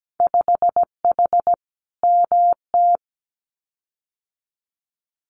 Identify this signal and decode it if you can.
Morse code, heard as a keyed tone, about 17 wpm, 700 Hz. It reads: 5H MT
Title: Winter Field Day Exchanges Encore 17wpm